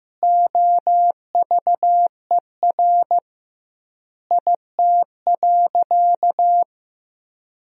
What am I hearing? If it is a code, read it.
Morse code, heard as a keyed tone, about 16 wpm, 700 Hz. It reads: OVER IT.
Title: 2x Word-Spacing - Sentences from Top 100 Words - 15wpm